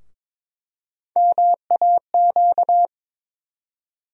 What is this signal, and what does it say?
Morse code, heard as a keyed tone, about 22 wpm, 700 Hz. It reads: MAQ
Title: USA Call Sign Postfixes - 3 Letters 22wpm